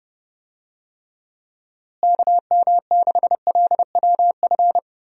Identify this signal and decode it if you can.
Morse code, heard as a keyed tone, about 30 wpm, 700 Hz. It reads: KM6LWF
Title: USA Call Signs - 6 Characters - Repeated 2x - 30wpm